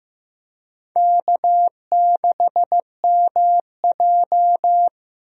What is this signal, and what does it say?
Morse code, heard as a keyed tone, about 15 wpm, 700 Hz. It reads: K6MJ
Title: USA Call Signs - 4 Characters - Repeated 3x - 15wpm